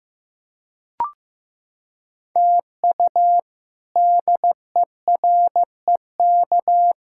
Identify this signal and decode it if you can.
Morse code, heard as a keyed tone, about 15 wpm, 700 Hz. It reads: TU DEREK
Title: K1USN SST QSO Practice - 15wpm